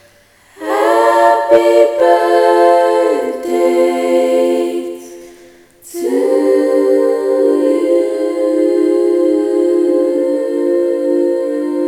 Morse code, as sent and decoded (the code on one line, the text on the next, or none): none